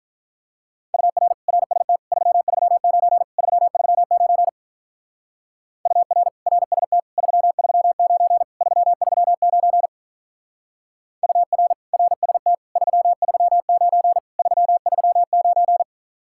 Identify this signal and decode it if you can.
Morse code, heard as a keyed tone, about 44 wpm, 700 Hz. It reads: UR RST 339 339 UR RST 339 339 UR RST 339 339